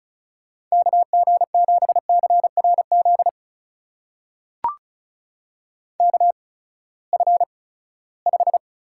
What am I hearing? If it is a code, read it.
Morse code, heard as a keyed tone, about 36 wpm, spaced as for 20 wpm, 700 Hz. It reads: KG7CRZ KF5
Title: USA Call Signs - 6 Characters - 8x Character Spacing - 35wpm